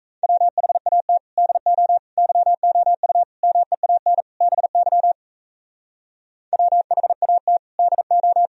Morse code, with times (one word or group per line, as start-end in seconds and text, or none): 0.2-1.2 WHAT
1.4-2.0 DO
2.2-3.2 YOU
3.4-4.2 MEAN
4.4-5.1 BY
6.5-7.6 WHAT
7.8-8.5 DO